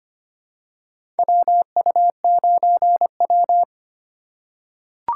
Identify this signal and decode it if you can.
Morse code, heard as a keyed tone, about 25 wpm, 700 Hz. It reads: WU9W
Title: USA Call Signs - 4 Characters - 2x Character Spacing - 25wpm